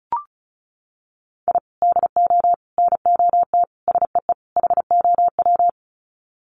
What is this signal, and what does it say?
Morse code, heard as a keyed tone, about 34 wpm, 700 Hz. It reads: I DO NOT SEE HOW
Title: Sets of 5 Words 35wpm